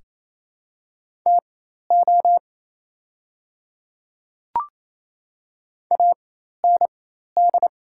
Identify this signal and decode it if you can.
Morse code, heard as a keyed tone, about 28 wpm, spaced as for 13 wpm, 700 Hz. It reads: TO AND
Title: 4x Character Spacing - Top 100 Words - 28wpm